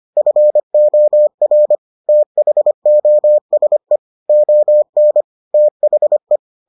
Morse code, heard as a keyed tone, about 25 wpm, 600 Hz. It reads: FOR THOSE ON THE